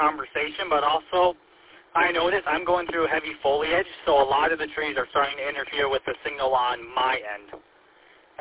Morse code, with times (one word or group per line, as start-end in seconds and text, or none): none